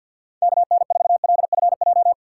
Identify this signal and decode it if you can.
Morse code, heard as a keyed tone, about 53 wpm, 700 Hz. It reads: KN4LFJ
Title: USA Call Signs - 2x Character Spacing - 50wpm